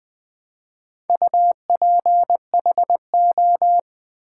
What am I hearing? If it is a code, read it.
Morse code, heard as a keyed tone, about 20 wpm, 700 Hz. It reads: UPHO
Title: Build A Word - 2 Letters At A Time - 6-Letter Words - 20wpm